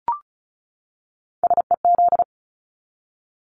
Morse code, heard as a keyed tone, about 34 wpm, 700 Hz. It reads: SEZ
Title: Abbreviations 35wpm